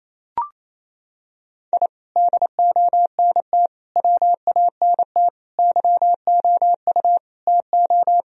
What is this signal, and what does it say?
Morse code, heard as a keyed tone, about 28 wpm, 700 Hz. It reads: I DONT WANT YOU TO
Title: Sentences from Top 1000 Words - Review Part 3 - 28wpm